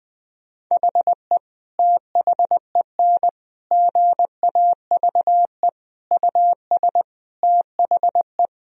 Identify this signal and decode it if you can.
Morse code, heard as a keyed tone, about 21 wpm, 700 Hz. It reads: HE THEN GAVE US THE
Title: Sentences from Top 1000 Words - Review Part 4 - 20wpm